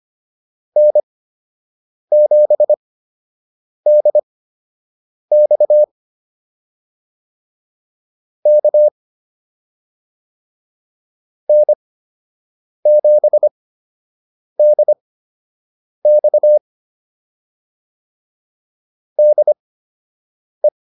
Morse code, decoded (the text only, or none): N7DX K N7DX DE